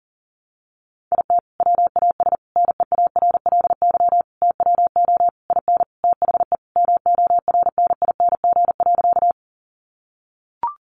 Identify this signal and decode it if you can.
Morse code, heard as a keyed tone, about 40 wpm, 700 Hz. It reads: IT WAS NEARLY TWO IN THE MORNING.
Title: Sentences from Top 1300 Words 40wpm